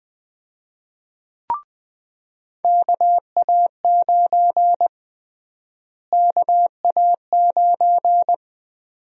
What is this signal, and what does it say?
Morse code, heard as a keyed tone, about 20 wpm, 700 Hz. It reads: KA9 KA9
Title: USA Call Sign Prefixes - Repeated 2x - 20wpm